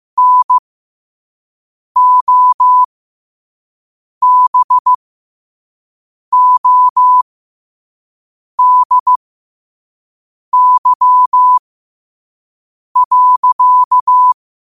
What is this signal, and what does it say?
Morse code, heard as a keyed tone, about 15 wpm, spaced as for 5 wpm, 1000 Hz. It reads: NOBODY.